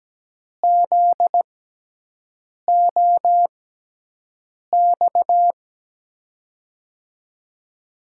Morse code, heard as a keyed tone, about 17 wpm, spaced as for 6 wpm, 700 Hz. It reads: ZOX
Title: USA Call Signs - 6x Character Spacing - 17wpm